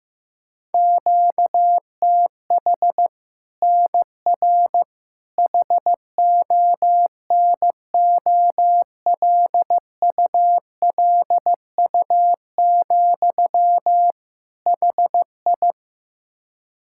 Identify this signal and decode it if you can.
Morse code, heard as a keyed tone, about 15 wpm, 700 Hz. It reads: QTH NR HONOLULU, HI
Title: Practice QSO - Exchange 1 - Beginning - 15wpm